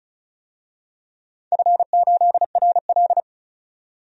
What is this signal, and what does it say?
Morse code, heard as a keyed tone, about 36 wpm, 700 Hz. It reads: F8RL
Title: DXCC-100 Call Signs - 4 Characters - 35wpm